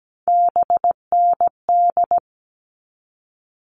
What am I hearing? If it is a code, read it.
Morse code, heard as a keyed tone, about 17 wpm, 700 Hz. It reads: BND
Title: All Cut Numbers - 3-Digit 17wpm